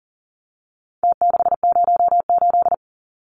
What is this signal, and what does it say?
Morse code, heard as a keyed tone, about 40 wpm, 700 Hz. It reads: T608
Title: Numbers - Contest Style - 4-Digit 40wpm